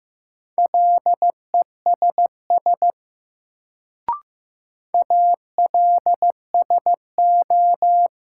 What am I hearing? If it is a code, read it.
Morse code, heard as a keyed tone, about 15 wpm, 700 Hz. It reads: LESS ALSO
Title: Sentences from Top 400 Words - New Word Drill - 15wpm